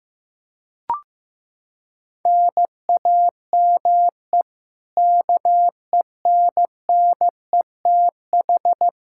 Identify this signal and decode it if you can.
Morse code, heard as a keyed tone, about 15 wpm, 700 Hz. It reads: NAME KENNETH
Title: Practice QSO - Exchange 1 - Beginning - 15wpm